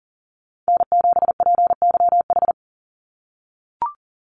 Morse code, heard as a keyed tone, about 40 wpm, 700 Hz. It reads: N7PYH
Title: USA Call Signs 40wpm